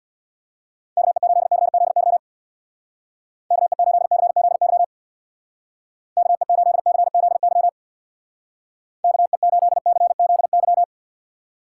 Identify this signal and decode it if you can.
Morse code, heard as a keyed tone, about 66 wpm, 700 Hz. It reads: KE8CZY KE8CZY KE8CZY KE8CZY